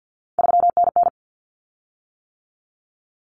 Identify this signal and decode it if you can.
Morse code, heard as a keyed tone, about 50 wpm, 700 Hz. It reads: 3NN
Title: Common Cut Numbers - 3-Digit 50wpm